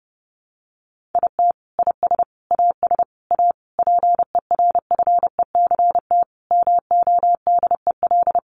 Morse code, heard as a keyed tone, about 30 wpm, 700 Hz. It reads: IT IS AS A PERFECT MODEL
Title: Sentences from Top 1000 Words 30wpm